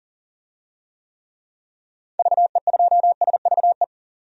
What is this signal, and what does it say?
Morse code, heard as a keyed tone, about 40 wpm, 700 Hz. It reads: VE2SVE
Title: DXCC-100 Call Signs - 6 Characters - 40wpm